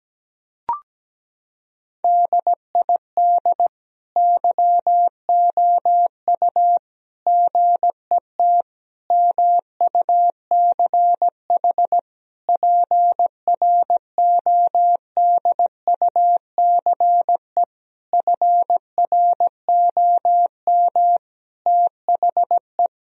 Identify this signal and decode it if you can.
Morse code, heard as a keyed tone, about 17 wpm, 700 Hz. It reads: DID YOU GET MUCH PRODUCE FROM THE
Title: Sentences from Top 1000 Words - Review Part 3 - 17wpm